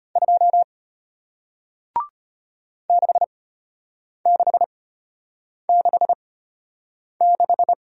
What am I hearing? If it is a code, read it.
Morse code, heard as a keyed tone, about 33 wpm, spaced as for 8 wpm, 700 Hz. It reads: 2 6666